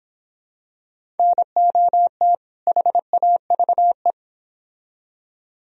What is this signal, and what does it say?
Morse code, heard as a keyed tone, about 26 wpm, 700 Hz. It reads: NOT HAVE